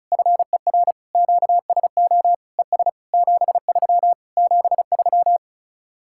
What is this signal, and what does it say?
Morse code, heard as a keyed tone, about 40 wpm, 700 Hz. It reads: FER QSO ES 73 73